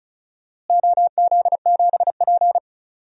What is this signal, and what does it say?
Morse code, heard as a keyed tone, about 36 wpm, 700 Hz. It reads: OZ7P
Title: DXCC-100 Call Signs - 4 Characters - Repeated 2x - 35wpm